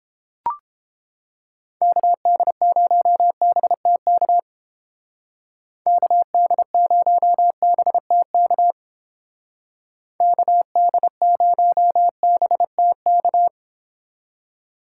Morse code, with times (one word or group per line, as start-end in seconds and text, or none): 1.8-4.4 KD0BTK
5.9-8.7 KD0BTK
10.2-13.5 KD0BTK